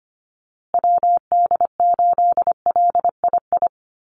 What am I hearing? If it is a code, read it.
Morse code, heard as a keyed tone, about 25 wpm, 700 Hz. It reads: WD8LII
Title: USA Call Signs Repeated 2x 25wpm